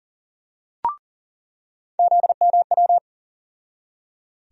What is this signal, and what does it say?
Morse code, heard as a keyed tone, about 38 wpm, 700 Hz. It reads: ZMW
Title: USA Call Sign Postfixes - 3 Letters 40wpm